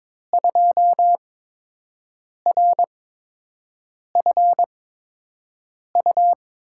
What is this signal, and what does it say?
Morse code, heard as a keyed tone, about 22 wpm, spaced as for 6 wpm, 700 Hz. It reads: 2RFU
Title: USA Call Signs - 8x Character Spacing - 22wpm